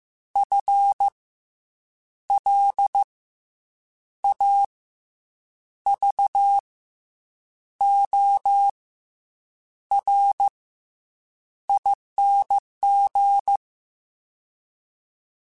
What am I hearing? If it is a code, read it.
Morse code, heard as a keyed tone, about 15 wpm, 800 Hz. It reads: F L A V O R ING
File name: Special INGs / Special INGs-mp3s / SlowToFaster ing @13